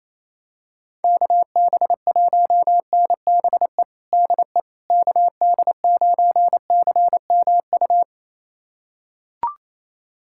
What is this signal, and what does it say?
Morse code, heard as a keyed tone, about 28 wpm, 700 Hz. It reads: KB1NBE DE KD9CMU
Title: USA Callsign DE Callsign - 6 Characters - 28wpm